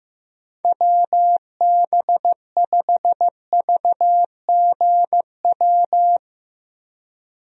Morse code, heard as a keyed tone, about 15 wpm, 700 Hz. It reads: WB5VGW
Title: USA Call Signs - 6 Characters - Repeated 3x - 15wpm